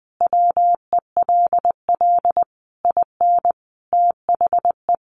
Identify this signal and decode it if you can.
Morse code, heard as a keyed tone, about 20 wpm, 700 Hz. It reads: WELL IN THE
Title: Sentences from Top 1200 Words 20wpm